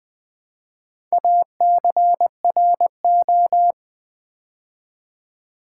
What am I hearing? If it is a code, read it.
Morse code, heard as a keyed tone, about 20 wpm, 700 Hz. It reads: ACRO
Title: Build A Word - 2 Letters At A Time - 6-Letter Words - 20wpm